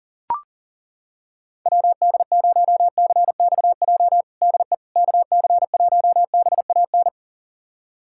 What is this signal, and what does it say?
Morse code, heard as a keyed tone, about 40 wpm, 700 Hz. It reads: WD0CXJ DE KC1BAN
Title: USA Callsign DE Callsign - 6 Characters - 40wpm